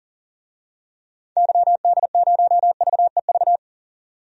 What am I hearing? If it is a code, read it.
Morse code, heard as a keyed tone, about 40 wpm, 700 Hz. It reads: YD0VEV
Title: DXCC-100 Call Signs - 6 Characters - 40wpm